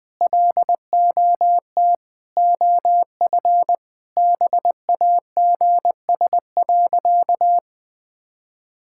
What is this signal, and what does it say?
Morse code, heard as a keyed tone, about 20 wpm, 700 Hz. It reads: LOT OF BAGS.
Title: Sentences with 4-Word Phrases - Spoken Beginning - 20wpm